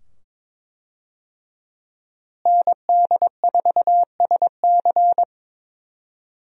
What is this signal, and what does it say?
Morse code, heard as a keyed tone, about 22 wpm, 700 Hz. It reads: ND4SC
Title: USA Call Signs - 5 Characters - Repeated 2x - 22wpm